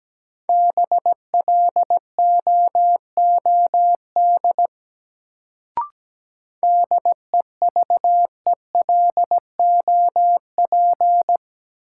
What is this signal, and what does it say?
Morse code, heard as a keyed tone, about 17 wpm, 700 Hz. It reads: BLOOD DEVELOP
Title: Sentences from Top 1000 Words - Review Part 2 - New Word Drill - 17wpm